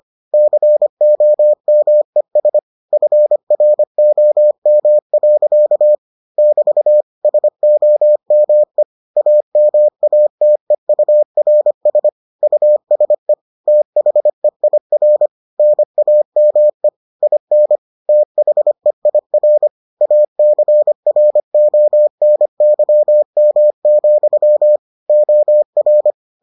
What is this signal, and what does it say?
Morse code, heard as a keyed tone, about 25 wpm, 600 Hz. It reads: COMES FROM. <BT> SOME AMATEURS USE THEIR NAME IN THEIR ACRONYM, OR